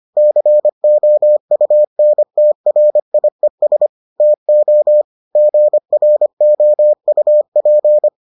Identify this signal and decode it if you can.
Morse code, heard as a keyed tone, about 25 wpm, 600 Hz. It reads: COUNTRIES TO GROUP